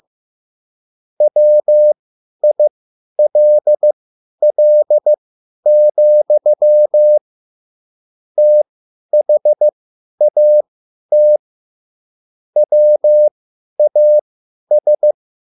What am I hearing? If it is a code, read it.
Morse code, heard as a keyed tone, about 15 wpm, spaced as for 10 wpm, 600 Hz. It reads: WILL, THAT WAS